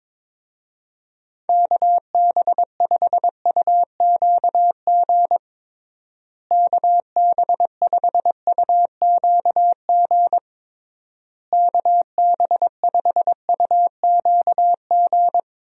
Morse code, heard as a keyed tone, about 22 wpm, 700 Hz. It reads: KB5UQG KB5UQG KB5UQG
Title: USA Call Signs - 6 Characters - Repeated 3x - 22wpm